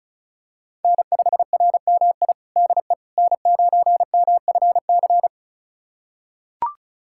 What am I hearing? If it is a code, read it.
Morse code, heard as a keyed tone, about 36 wpm, 700 Hz. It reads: N5RMI DE N9MFC
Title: USA Callsign DE Callsign - 5 Characters - 35wpm